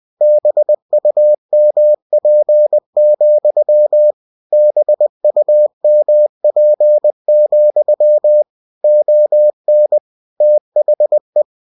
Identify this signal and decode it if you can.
Morse code, heard as a keyed tone, about 20 wpm, 600 Hz. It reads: BUMP, BUMP, ON THE
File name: cw-books / winnie-the-pooh / winnie-20wpm-page-0000